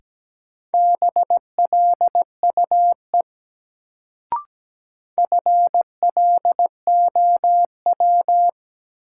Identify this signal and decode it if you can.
Morse code, heard as a keyed tone, about 17 wpm, 700 Hz. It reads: BLUE FLOW
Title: Sentences from Top 1200 Words - New Word Drill Alt - 17wpm